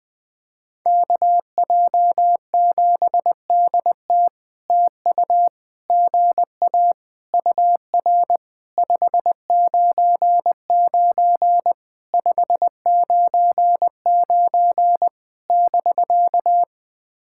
Morse code, heard as a keyed tone, about 20 wpm, 700 Hz. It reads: KJ7DT TU GA UR 599 599 <BK>
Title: POTA QSO Practice - 20wpm